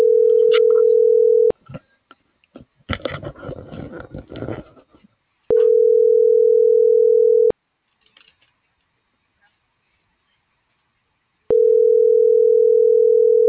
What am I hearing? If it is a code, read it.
Ringback tone (North America), 440 + 480 Hz.